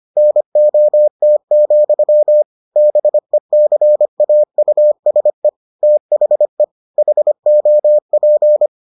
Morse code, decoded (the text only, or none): NOT, BECAUSE THE HOP